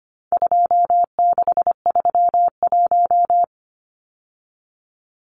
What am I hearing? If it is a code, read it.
Morse code, heard as a keyed tone, about 25 wpm, 700 Hz. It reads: 2631
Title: Numbers - 4-Digit 25wpm